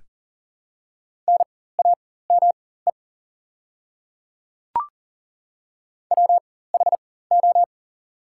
Morse code, heard as a keyed tone, about 38 wpm, spaced as for 18 wpm, 700 Hz. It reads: NAME WHO
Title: 4x Character Spacing - Top 200 Words - 40wpm